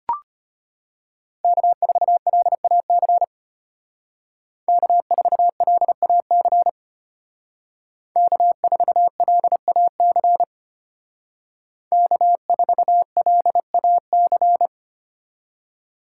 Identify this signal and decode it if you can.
Morse code, heard as a keyed tone, about 33 wpm, 700 Hz. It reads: K4LAC K4LAC K4LAC K4LAC